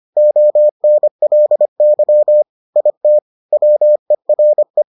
Morse code, heard as a keyed tone, about 25 wpm, 600 Hz. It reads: ONLY IT WERE